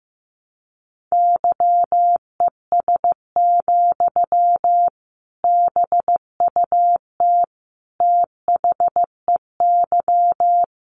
Morse code, heard as a keyed tone, about 15 wpm, 700 Hz. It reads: YES, BUT THEY
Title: Sentences from Top 600 Words 15wpm